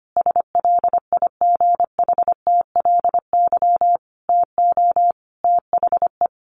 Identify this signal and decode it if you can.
Morse code, heard as a keyed tone, about 25 wpm, 700 Hz. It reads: SLIGHTLY TO THE
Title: Sentences from Top 1300 Words 25wpm